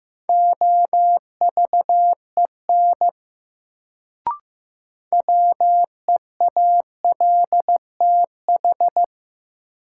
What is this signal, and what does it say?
Morse code, heard as a keyed tone, about 15 wpm, 700 Hz. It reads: OVEN WEALTH
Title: Sentences from Top 1000 Words - Review Part 4 - New Word Drill - 15wpm